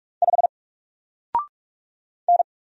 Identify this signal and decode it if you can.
Morse code, heard as a keyed tone, about 44 wpm, 700 Hz. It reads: 5 N
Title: Single Character (Letter:Number:Punctuation:Prosign) 45wpm